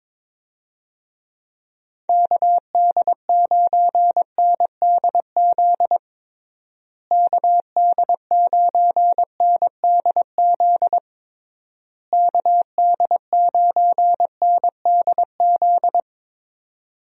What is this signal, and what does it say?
Morse code, heard as a keyed tone, about 21 wpm, 700 Hz. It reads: KD9NDZ KD9NDZ KD9NDZ
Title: USA Call Signs - 6 Characters - Repeated 3x - 22wpm